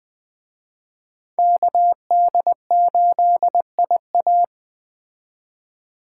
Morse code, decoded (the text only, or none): KD8IA